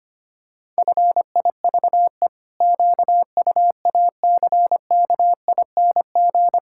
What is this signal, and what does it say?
Morse code, heard as a keyed tone, about 25 wpm, 700 Hz. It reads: FIVE QUACKING
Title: Pangram Challenge v2 - 25wpm